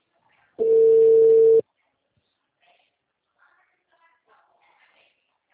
Ringback tone (China, 450 Hz).
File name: out- -2011- - - .70